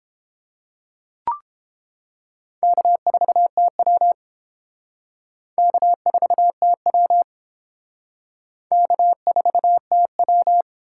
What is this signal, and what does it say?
Morse code, heard as a keyed tone, about 29 wpm, 700 Hz. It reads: K4TW K4TW K4TW